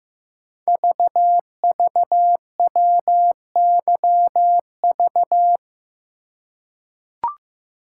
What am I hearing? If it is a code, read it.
Morse code, heard as a keyed tone, about 15 wpm, 700 Hz. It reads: VVWYV
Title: Code Groups - 5 Letters - 15wpm